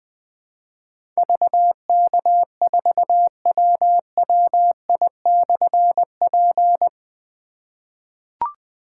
Morse code, heard as a keyed tone, about 20 wpm, 700 Hz. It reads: VK4WWI/P
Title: Difficult Call Signs 20wpm